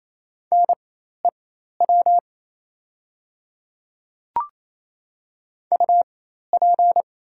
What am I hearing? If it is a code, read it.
Morse code, heard as a keyed tone, about 28 wpm, spaced as for 13 wpm, 700 Hz. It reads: NEW UP
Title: 4x Character Spacing - Top 500 Words - 28wpm